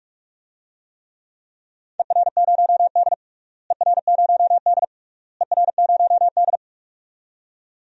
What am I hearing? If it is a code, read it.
Morse code, heard as a keyed tone, about 46 wpm, 700 Hz. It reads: ER0D ER0D ER0D
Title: DXCC-100 Call Signs - 4 Characters - Repeated 3x - 45wpm